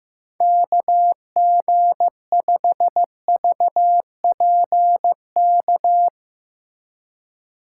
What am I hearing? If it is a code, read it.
Morse code, heard as a keyed tone, about 15 wpm, 700 Hz. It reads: KG5VPK
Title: USA Call Signs - 2x Character Spacing - 15wpm